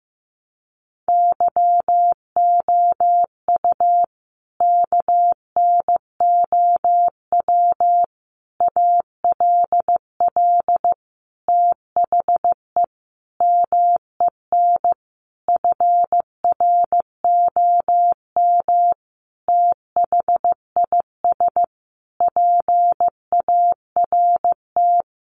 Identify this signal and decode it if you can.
Morse code, heard as a keyed tone, about 15 wpm, 700 Hz. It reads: YOU KNOW ALL THE MEN FROM THIS PART
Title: Sentences from Top 300 Words 15wpm